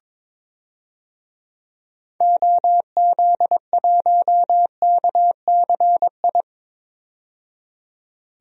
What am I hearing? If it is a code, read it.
Morse code, heard as a keyed tone, about 22 wpm, 700 Hz. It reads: OZ1KCI
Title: DXCC-100 Call Signs - 6 Characters - 22wpm